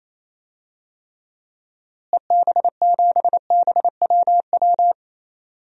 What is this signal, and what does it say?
Morse code, heard as a keyed tone, about 28 wpm, 700 Hz. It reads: EB7BWW
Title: DXCC-100 Call Signs - 6 Characters - Repeated 3x - 28wpm